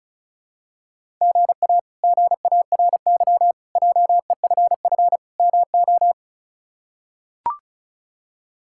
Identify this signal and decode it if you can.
Morse code, heard as a keyed tone, about 36 wpm, 700 Hz. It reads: GA GARY JEFF MO
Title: K1USN SST QSO Practice - 35wpm